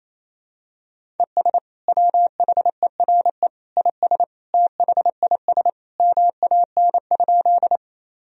Morse code, heard as a keyed tone, about 28 wpm, 700 Hz. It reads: ES WHERE IS THIS MAN?
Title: Sentences from Top 200 Words and Abbreviations 28wpm